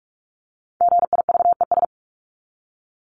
Morse code, heard as a keyed tone, about 44 wpm, 700 Hz. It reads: GIVES
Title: ES:S Suffix - Top 100 Words with Base Words 45wpm